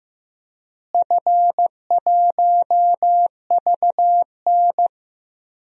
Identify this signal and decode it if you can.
Morse code, heard as a keyed tone, about 15 wpm, 700 Hz. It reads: F1VN
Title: DXCC-100 Call Signs - 4 Characters - Repeated 3x - 15wpm